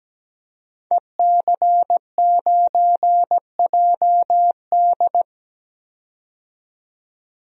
Morse code, heard as a keyed tone, about 17 wpm, 700 Hz. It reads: EC9JD
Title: DXCC-100 Call Signs - 5 Characters - 17wpm